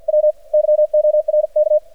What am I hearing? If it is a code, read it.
Morse code, heard as a keyed tone, about 53 wpm, spaced as for 46 wpm, 600 Hz. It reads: W YORK
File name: archived / CRSnet